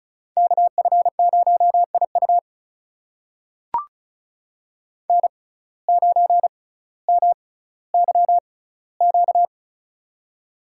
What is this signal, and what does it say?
Morse code, heard as a keyed tone, about 36 wpm, spaced as for 18 wpm, 700 Hz. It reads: KF0IU N9MYQ
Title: USA Call Signs - 5 Characters - 6x Character Spacing - 35wpm